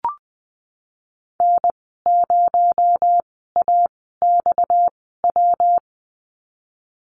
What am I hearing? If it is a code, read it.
Morse code, heard as a keyed tone, about 20 wpm, spaced as for 15 wpm, 700 Hz. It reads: N0AXW
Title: 231 Lesson 37g - Letter x - Call Signs 20wpm